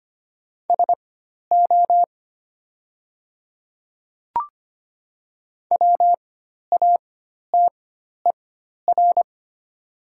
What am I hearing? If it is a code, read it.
Morse code, heard as a keyed tone, about 25 wpm, spaced as for 12 wpm, 700 Hz. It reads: SO WATER
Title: 4x Character Spacing - Top 100 Words - 25wpm